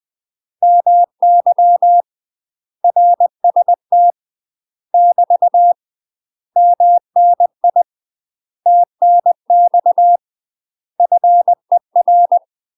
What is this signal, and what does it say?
Morse code, heard as a keyed tone, about 20 wpm, 700 Hz. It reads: MY RST <BT> MNI TNX FER
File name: UK MASTER QSOs TRAINING / 20WPM / UK MASTER QSOs 0025